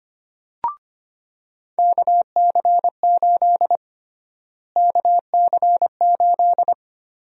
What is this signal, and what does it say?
Morse code, heard as a keyed tone, about 25 wpm, 700 Hz. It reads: KC8 KC8
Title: USA Call Sign Prefixes - Repeated 2x - 25wpm